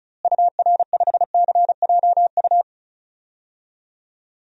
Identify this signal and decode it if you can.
Morse code, heard as a keyed tone, about 34 wpm, 700 Hz. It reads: UR5CJU
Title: DXCC-100 Call Signs - 6 Characters - Repeated 2x - 35wpm